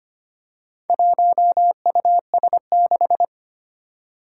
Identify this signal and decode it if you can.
Morse code, heard as a keyed tone, about 25 wpm, 700 Hz. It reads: 1US6
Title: Four Challenging Letter-Number Combinations 25wpm